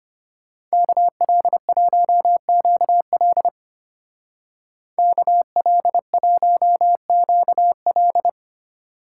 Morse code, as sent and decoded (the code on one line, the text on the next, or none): -.- .-.. .---- --.- .-.. / -.- .-.. .---- --.- .-..
KL1QL KL1QL